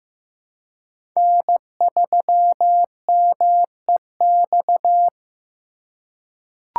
Morse code, heard as a keyed tone, about 15 wpm, 700 Hz. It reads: N3MEX
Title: USA Call Signs - 3x Character Spacing - 15wpm